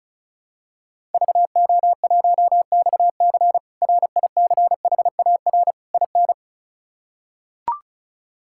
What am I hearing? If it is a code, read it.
Morse code, heard as a keyed tone, about 34 wpm, 700 Hz. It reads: VO1XC RICHAR IN
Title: K1USN SST QSO Practice - 35wpm